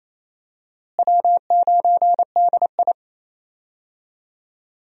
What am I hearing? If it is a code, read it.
Morse code, heard as a keyed tone, about 28 wpm, 700 Hz. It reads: W9DI
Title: USA Call Signs - 4 Characters - Repeated 3x - 28wpm